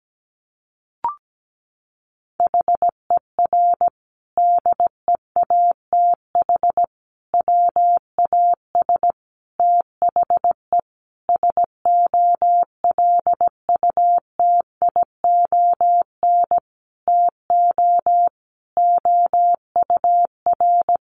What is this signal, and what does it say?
Morse code, heard as a keyed tone, about 17 wpm, 700 Hz. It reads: HER DEATH WAS THE SOLUTION TO OUR
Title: Sentences from Top 1300 Words 17wpm